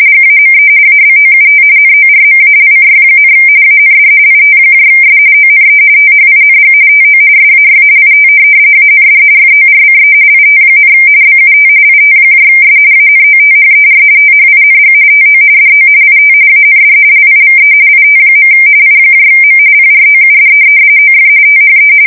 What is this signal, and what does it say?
RTTY, about 45.45 baud. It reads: GAME ABOUT NORMALCY, AND ITS ONE THING TO SAY IT AND FORCE IT, BUT ITS ANOTHER THING TO DO IT AND LIVE IT. IM PROUD OF THE
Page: 170 Shift 2125/2295